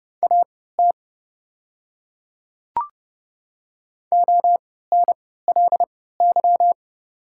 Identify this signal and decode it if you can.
Morse code, heard as a keyed tone, about 29 wpm, spaced as for 17 wpm, 700 Hz. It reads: AT ONLY
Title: 3x Character Spacing - Top 200 Words - 30wpm